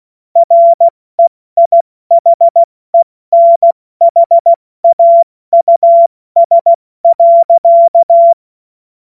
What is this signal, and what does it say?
Morse code, heard as a keyed tone, about 16 wpm, spaced as for 14 wpm, 680 Hz. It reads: REIHENHAUS.
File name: Morsekurs / Witze / Witz 15. August 2023 70 F80